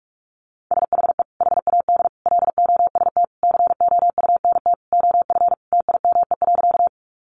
Morse code, heard as a keyed tone, about 44 wpm, 700 Hz. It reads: SHE HAD LOST COUNT OF TIME.
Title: Sentences from Top 1500 Words 45wpm